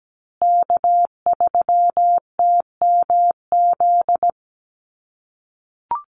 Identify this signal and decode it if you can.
Morse code, heard as a keyed tone, about 17 wpm, 700 Hz. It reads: K3TMZ
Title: USA Call Signs Repeated 2x 17wpm